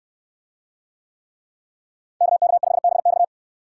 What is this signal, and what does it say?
Morse code, heard as a keyed tone, about 65 wpm, 700 Hz. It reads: KK5BX